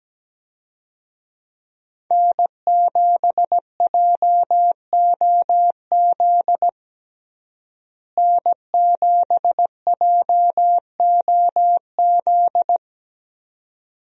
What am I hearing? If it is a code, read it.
Morse code, heard as a keyed tone, about 17 wpm, 700 Hz. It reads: N7JOZ N7JOZ
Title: USA Call Signs - 5 Characters - Repeated 2x - 17wpm